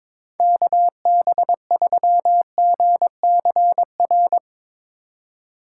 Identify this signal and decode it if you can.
Morse code, heard as a keyed tone, about 24 wpm, 700 Hz. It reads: KB3GCR
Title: Speed Racing - USA 2x3 Call Signs 22wpm